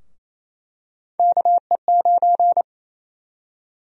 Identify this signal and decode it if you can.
Morse code, heard as a keyed tone, about 28 wpm, 700 Hz. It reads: KE9
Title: USA Call Sign Prefixes - 28wpm